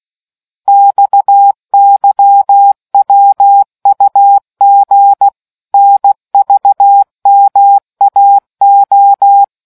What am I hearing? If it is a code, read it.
Morse code, heard as a keyed tone, about 16 wpm, 800 Hz. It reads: XYWUG NVMAO